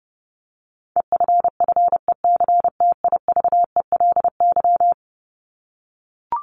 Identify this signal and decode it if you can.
Morse code, heard as a keyed tone, about 30 wpm, 700 Hz. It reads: EFFECTIVELY
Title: Top 1251-1500 Words 30wpm